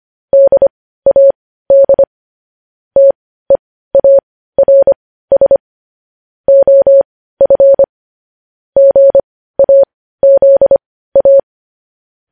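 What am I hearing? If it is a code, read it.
Morse code, heard as a keyed tone, about 25 wpm, spaced as for 15 wpm, 550 Hz. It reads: DAD TEARS OF GAZA
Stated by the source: Morse code news headlines at 15 WPM.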